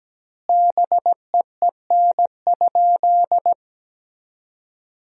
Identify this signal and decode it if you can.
Morse code, heard as a keyed tone, about 17 wpm, 700 Hz. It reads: BEEN?
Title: 2x Word-Spacing - Sentences from Top 200 Words - 17wpm